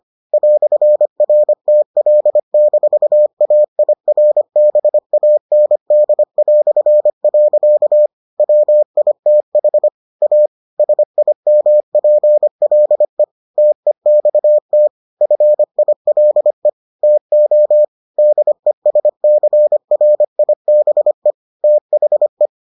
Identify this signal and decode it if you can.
Morse code, heard as a keyed tone, about 25 wpm, 600 Hz. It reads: "RTL-AIRBAND". WITH A SIMPLE TEXT FILE TO DESCRIBE THE